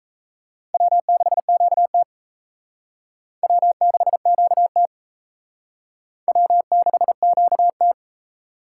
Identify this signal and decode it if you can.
Morse code, heard as a keyed tone, about 32 wpm, 700 Hz. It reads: W6QT W6QT W6QT
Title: Speed Racing - USA Call Signs 28wpm